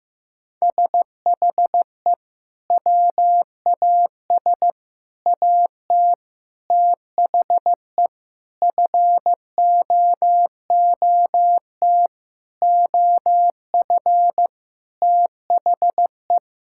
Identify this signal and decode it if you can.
Morse code, heard as a keyed tone, about 15 wpm, 700 Hz. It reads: SHE WAS AT THE FOOT OF THE
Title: Sentences from Top 1000 Words - Review Part 5 - 15wpm